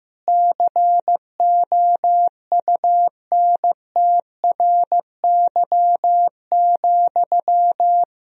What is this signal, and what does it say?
Morse code, heard as a keyed tone, about 15 wpm, 700 Hz. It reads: COUNTRY,
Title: Sentences from Top 1000 Words - Review Part 3 - 15wpm